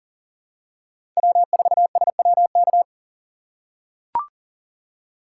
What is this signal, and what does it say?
Morse code, heard as a keyed tone, about 40 wpm, 700 Hz. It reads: W4SWK
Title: USA Call Signs - 2x Character Spacing - 40wpm